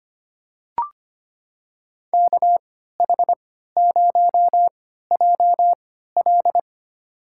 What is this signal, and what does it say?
Morse code, heard as a keyed tone, about 25 wpm, spaced as for 14 wpm, 700 Hz. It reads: KH0JL
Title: USA Call Signs - 5 Characters - 3x Character Spacing - 25wpm